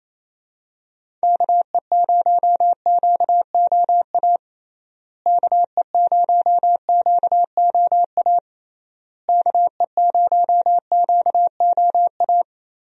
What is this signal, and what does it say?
Morse code, heard as a keyed tone, about 28 wpm, 700 Hz. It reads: KE0QOA KE0QOA KE0QOA
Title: USA Call Signs - 6 Characters - Repeated 3x - 28wpm